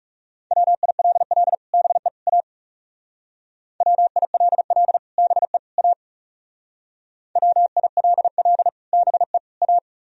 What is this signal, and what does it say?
Morse code, heard as a keyed tone, about 40 wpm, 700 Hz. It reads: WILL BE A WILL BE A WILL BE A